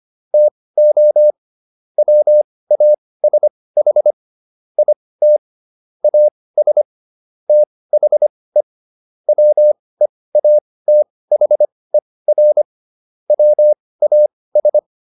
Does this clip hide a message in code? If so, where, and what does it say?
Yes: Morse code from 0.3 to 14.8 s, heard as a keyed tone, about 25 wpm, spaced as for 18 wpm, 600 Hz. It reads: TO WASH IT AS THE WEATHER WAS